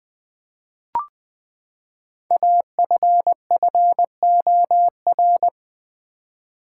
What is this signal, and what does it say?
Morse code, heard as a keyed tone, about 20 wpm, 700 Hz. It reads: AFFOR
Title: Build A Word - 5 Letters At A Time - 10-Letter Words - 20wpm